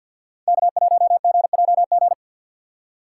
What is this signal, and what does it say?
Morse code, heard as a keyed tone, about 51 wpm, 700 Hz. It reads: K1GJG
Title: USA Call Signs - 5 Characters - 4x Character Spacing - 50wpm